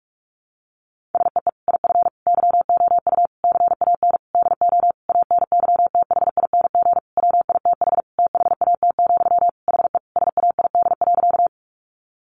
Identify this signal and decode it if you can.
Morse code, heard as a keyed tone, about 44 wpm, 700 Hz. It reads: SEE IF YOU CAN DO ANYTHING WITH THAT, HE SAID.
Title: Sentences from Top 1000 Words 45wpm